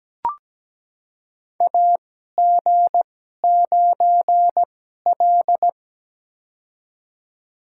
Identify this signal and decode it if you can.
Morse code, heard as a keyed tone, about 17 wpm, spaced as for 12 wpm, 700 Hz. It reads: AG9L
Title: USA Call Signs - 4 Characters - 2x Character Spacing - 17wpm